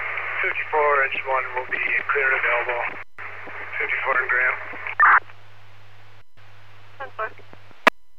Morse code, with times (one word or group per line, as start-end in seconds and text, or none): none